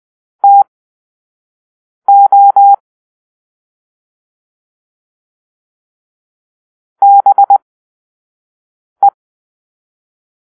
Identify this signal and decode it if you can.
Morse code, heard as a keyed tone, about 20 wpm, spaced as for 5 wpm, 800 Hz. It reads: TO BE
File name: arrl / Grp-5 Drill-4